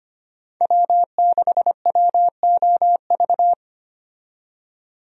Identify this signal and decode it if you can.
Morse code, heard as a keyed tone, about 25 wpm, 700 Hz. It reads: W6WOV
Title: USA Call Signs - 5 Characters - 25wpm